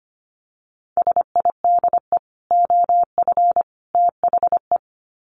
Morse code, heard as a keyed tone, about 25 wpm, 700 Hz. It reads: SIDE OF THE
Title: Sets of 3 Words 25wpm